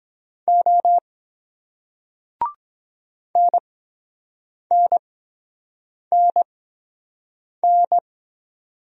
Morse code, heard as a keyed tone, about 22 wpm, spaced as for 7 wpm, 700 Hz. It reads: O NNNN